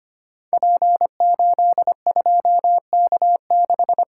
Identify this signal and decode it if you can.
Morse code, heard as a keyed tone, about 25 wpm, 700 Hz. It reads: P82K6
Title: Code Groups - 5 Letter:Numbers - 25wpm